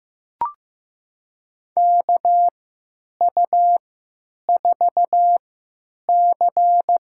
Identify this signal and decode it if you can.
Morse code, heard as a keyed tone, about 15 wpm, spaced as for 9 wpm, 700 Hz. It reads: KU4C
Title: USA Call Signs - 3x Character Spacing - 15wpm